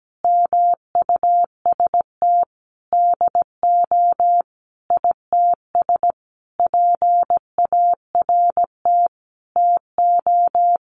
Morse code, heard as a keyed tone, about 17 wpm, 700 Hz. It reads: MUST DO ITS PART TO
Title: Sentences from Top 1100 Words 17wpm